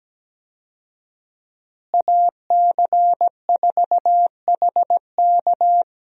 Morse code, heard as a keyed tone, about 17 wpm, 700 Hz. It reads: AC4HK
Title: USA Call Signs - 5 Characters - Repeated 2x - 17wpm